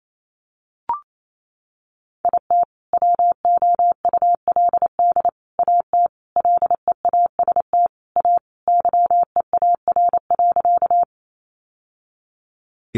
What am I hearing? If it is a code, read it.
Morse code, heard as a keyed tone, about 28 wpm, 700 Hz. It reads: IT WOULD AT LEAST A YEAR.
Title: Sentences from Top 700 Words 28wpm